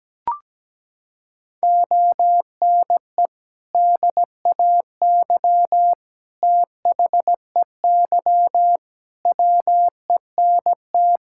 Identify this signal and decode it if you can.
Morse code, heard as a keyed tone, about 17 wpm, 700 Hz. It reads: ONE DAY THEY WENT
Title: Sentences from Top 1600 Words 17wpm